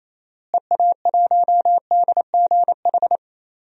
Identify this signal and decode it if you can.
Morse code, heard as a keyed tone, about 28 wpm, 700 Hz. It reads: EA1DGH
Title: DXCC-100 Call Signs - 6 Characters - Repeated 3x - 28wpm